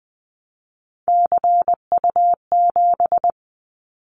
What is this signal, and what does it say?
Morse code, heard as a keyed tone, about 20 wpm, 700 Hz. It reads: CU7
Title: Callsign Prefixes for Top 100 DXCC - 20wpm